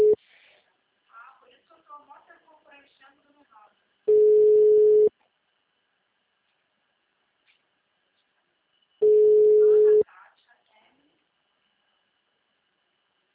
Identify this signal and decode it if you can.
Ringback tone (Europe, 425 Hz), with speech.
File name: out- -2002- - - .64439